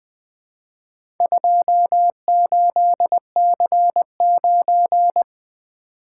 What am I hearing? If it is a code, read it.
Morse code, heard as a keyed tone, about 20 wpm, 700 Hz. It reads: 28C9
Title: Four Letter-Number Combinations - 20wpm